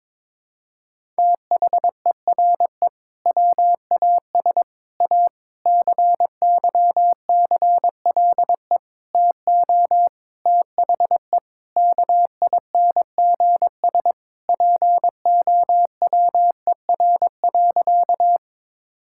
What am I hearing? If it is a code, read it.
Morse code, heard as a keyed tone, about 22 wpm, 700 Hz. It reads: THERE WAS A CYCLE TO THE KINGS POWER.
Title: Sentences from Top 1000 Words - Review Part 6 - 22wpm